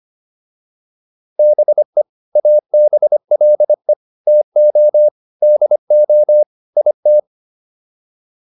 Morse code, heard as a keyed tone, about 25 wpm, 600 Hz. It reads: BE ABLE TO DO IT